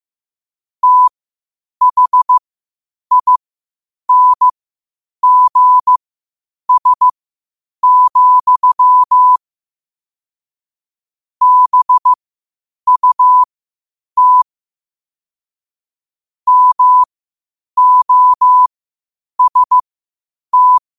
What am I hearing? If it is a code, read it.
Morse code, heard as a keyed tone, about 15 wpm, spaced as for 8 wpm, 1000 Hz. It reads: THINGS, BUT MOST
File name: qotdfiles / 8wpm